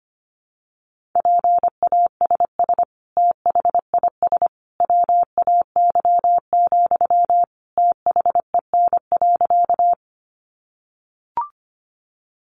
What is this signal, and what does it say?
Morse code, heard as a keyed tone, about 25 wpm, 700 Hz. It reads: PASS THIS WAY, THEN.
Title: Sentences from Top 400 Words 25wpm